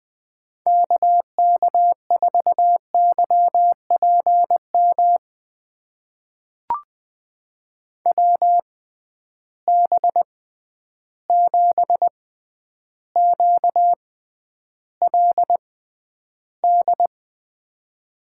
Morse code, heard as a keyed tone, about 20 wpm, spaced as for 10 wpm, 700 Hz. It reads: KK4YPM WB7QLD
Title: USA Call Signs - 6 Characters - 6x Character Spacing - 20wpm